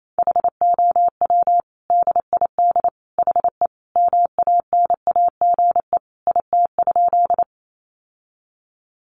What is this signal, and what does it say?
Morse code, heard as a keyed tone, about 28 wpm, 700 Hz. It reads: HOW DID HE MANAGE IT?
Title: Sentences from Top 1400 Words 28wpm